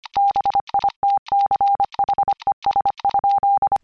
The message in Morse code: BITCHES?